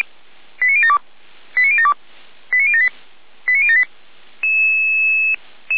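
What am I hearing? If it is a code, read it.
Five-tone selective call (ZVEI), heard as quick runs of five tones, tones of about 70 ms. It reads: 89072 89072 89078 89078
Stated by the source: ZVEI five-tone selective call received by SDR.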